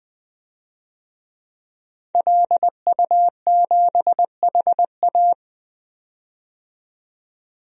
Morse code, heard as a keyed tone, about 20 wpm, 700 Hz. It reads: LU7HA
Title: DXCC-100 Call Signs - 5 Characters - 20wpm